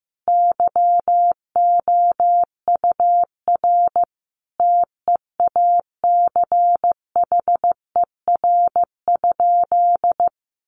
Morse code, heard as a keyed tone, about 15 wpm, 700 Hz. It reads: YOUR TEACHER?
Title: Sentences from Top 1300 Words 15wpm